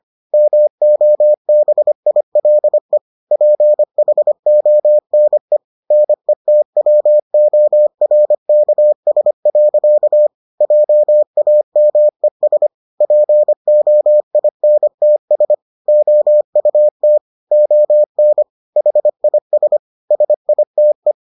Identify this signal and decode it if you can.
Morse code, heard as a keyed tone, about 25 wpm, 600 Hz. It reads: MOBILE PHONE NETWORKS. JAMES POINTS OUT ON HIS SITE